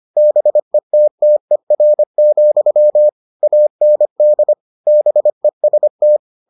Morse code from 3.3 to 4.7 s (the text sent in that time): AND